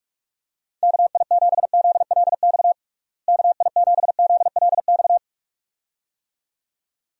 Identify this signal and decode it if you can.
Morse code, heard as a keyed tone, about 46 wpm, 700 Hz. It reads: KI7ZLX KI7ZLX
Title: USA Call Signs - 6 Characters - Repeated 2x - 45wpm